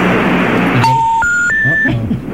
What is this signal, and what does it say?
Special information tone.
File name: mp3files / off the hook -128